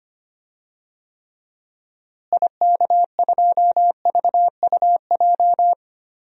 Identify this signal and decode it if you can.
Morse code, heard as a keyed tone, about 25 wpm, 700 Hz. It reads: IK2VUJ
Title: DXCC-100 Call Signs - 6 Characters - Repeated 3x - 25wpm